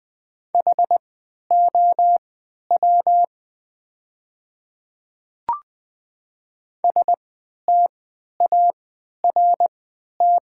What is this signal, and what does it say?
Morse code, heard as a keyed tone, about 20 wpm, spaced as for 11 wpm, 700 Hz. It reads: HOW START
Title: 3x Character Spacing - Top 300 Words - 20wpm